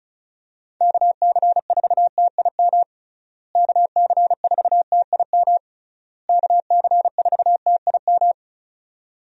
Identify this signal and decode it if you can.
Morse code, heard as a keyed tone, about 36 wpm, 700 Hz. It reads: KC4TIM KC4TIM KC4TIM
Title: USA Call Signs - 6 Characters - Repeated 3x - 35wpm